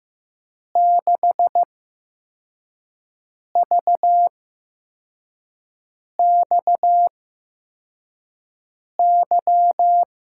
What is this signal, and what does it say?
Morse code, heard as a keyed tone, about 16 wpm, spaced as for 4 wpm, 700 Hz. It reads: 6VXY
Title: USA Call Signs - 8x Character Spacing - 15wpm